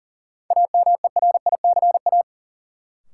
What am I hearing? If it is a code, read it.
Morse code, heard as a keyed tone, about 40 wpm, 700 Hz. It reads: AMERICA
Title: Sets of 5 Words - New Word Drill - 40wpm